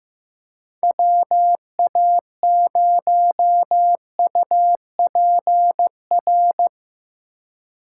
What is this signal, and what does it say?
Morse code, heard as a keyed tone, about 19 wpm, 700 Hz. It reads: WA0UPR
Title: Speed Racing - USA Call Signs 15wpm